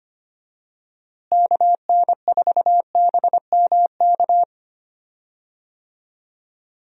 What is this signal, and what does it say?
Morse code, heard as a keyed tone, about 25 wpm, 700 Hz. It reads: KN4BMK
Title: USA Call Signs - 6 Characters - Repeated 3x - 25wpm